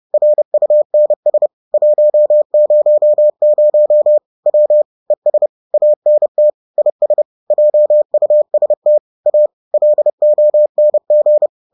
Morse code, heard as a keyed tone, about 30 wpm, 600 Hz. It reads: RUNS 100 W ES ANT IS JUST A LONG
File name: 30wpm-4